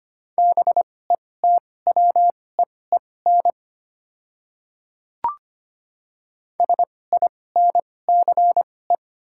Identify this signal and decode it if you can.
Morse code, heard as a keyed tone, about 25 wpm, spaced as for 18 wpm, 700 Hz. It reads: BETWEEN SINCE
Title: 2x Character Spacing - Top 500 Words - 25wpm